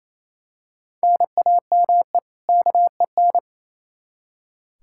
Morse code, heard as a keyed tone, about 28 wpm, 700 Hz. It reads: NAME KEN
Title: Practice QSO - Exchange 1 - Beginning - 28wpm